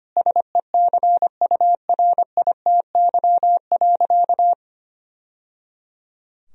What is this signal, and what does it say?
Morse code, heard as a keyed tone, about 25 wpm, 700 Hz. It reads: SECURITY.
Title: Sentences from Top 1000 Words - Review Part 4 - 25wpm